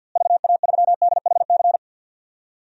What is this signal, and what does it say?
Morse code, heard as a keyed tone, about 52 wpm, 700 Hz. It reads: VA3DHC
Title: DXCC-100 Call Signs - 6 Characters - Repeated 2x - 50wpm